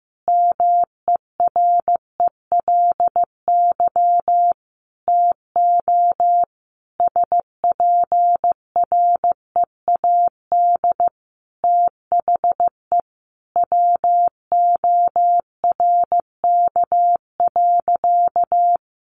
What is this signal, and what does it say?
Morse code, heard as a keyed tone, about 15 wpm, 700 Hz. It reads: MERELY TO SPREAD THE WORK.
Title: Sentences from Top 1400 Words 15wpm